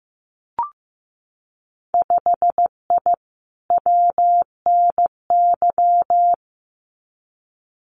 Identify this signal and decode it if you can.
Morse code, heard as a keyed tone, about 15 wpm, 700 Hz. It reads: 5I WNY
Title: Winter Field Day Exchanges Encore 15wpm